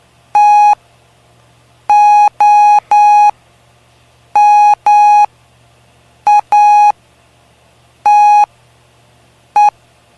Morse code, heard as a keyed tone, about 9 wpm, spaced as for 6 wpm, 830 Hz. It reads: TOMATE
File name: cw / py2lcd / 05-PY2LCD-Li��o 5